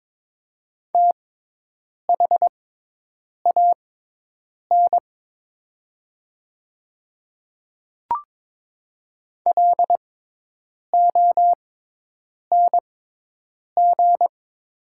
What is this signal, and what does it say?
Morse code, heard as a keyed tone, about 22 wpm, spaced as for 8 wpm, 700 Hz. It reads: THAN LONG
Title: 6x Character Spacing - Top 100 Words - 22wpm